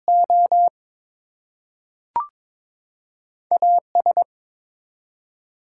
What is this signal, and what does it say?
Morse code, heard as a keyed tone, about 22 wpm, 700 Hz. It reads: O AS
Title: Two Letter Words - 22wpm